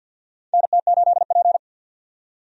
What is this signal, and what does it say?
Morse code, heard as a keyed tone, about 53 wpm, 700 Hz. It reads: NT8P
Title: USA Call Signs - 4 Characters - 50wpm